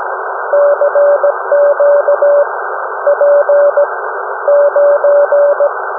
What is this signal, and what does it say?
Morse code, heard as a keyed tone, about 17 wpm, spaced as for 12 wpm, 580 Hz, with noise. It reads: CQ P9
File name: UM01- -0743utc-3581khz